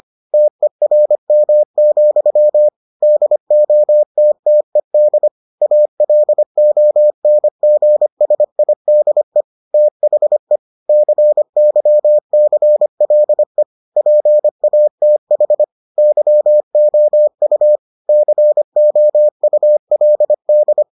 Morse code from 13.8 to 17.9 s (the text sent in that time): PATH YOU